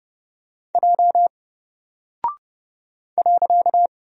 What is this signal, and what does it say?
Morse code, heard as a keyed tone, about 30 wpm, 700 Hz. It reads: J .
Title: Single Character (Letter:Number:Punctuation:Prosign) Alt 30wpm